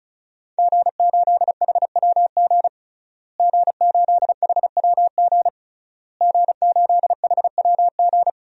Morse code, heard as a keyed tone, about 36 wpm, 700 Hz. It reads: G8HWG G8HWG G8HWG
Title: DXCC-100 Call Signs - 5 Characters - Repeated 3x - 35wpm